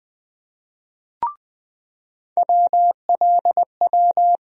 Morse code, heard as a keyed tone, about 20 wpm, 700 Hz. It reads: WLW
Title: USA Call Sign Postfixes - 3 Letters 20wpm